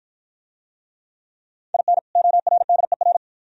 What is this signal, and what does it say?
Morse code, heard as a keyed tone, about 53 wpm, 700 Hz. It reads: IN ORDER